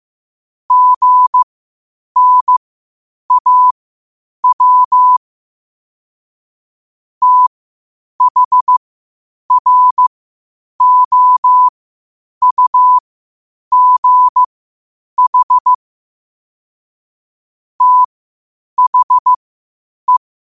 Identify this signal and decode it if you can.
Morse code, heard as a keyed tone, about 15 wpm, spaced as for 8 wpm, 1000 Hz. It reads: GNAW THROUGH THE